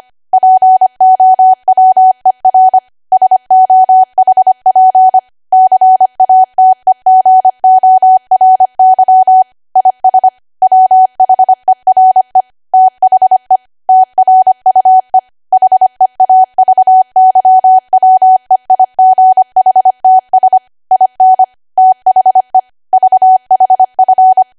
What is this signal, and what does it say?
Morse code, heard as a keyed tone, about 25 wpm, 750 Hz. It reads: POWER SOHP CATEGORY IS WHERE THE TRUE HEAVYWEIGHTS IN THE VHF